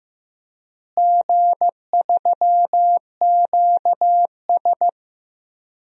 Morse code, heard as a keyed tone, about 15 wpm, 700 Hz. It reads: G3QS
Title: DXCC-100 Call Signs - 4 Characters - Repeated 2x - 15wpm